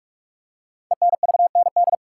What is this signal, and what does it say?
Morse code, heard as a keyed tone, about 44 wpm, 700 Hz. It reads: ENVND
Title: Code Groups - 5 Letters - 45wpm